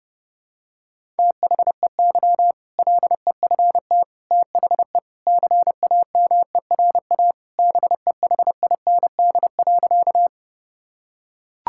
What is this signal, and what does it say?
Morse code, heard as a keyed tone, about 29 wpm, 700 Hz. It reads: THEY LEFT THE CAMERA BEHIND.
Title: Sentences from Top 1000 Words - Review Part 5 - 30wpm